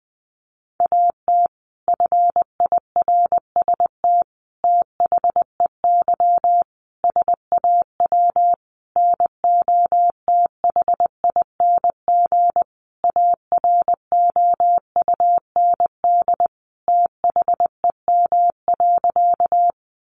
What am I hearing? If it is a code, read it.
Morse code, heard as a keyed tone, about 20 wpm, 700 Hz. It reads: AT FIRST THEY SAW NOTHING AROUND THEM.
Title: Sentences from Top 600 Words 20wpm